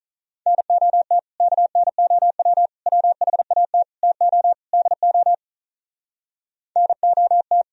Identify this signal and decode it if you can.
Morse code, heard as a keyed tone, about 43 wpm, 700 Hz. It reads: NOT KNOW WHAT TO DO NOT
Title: Speed-Racing - Sets of 5 Words 35wpm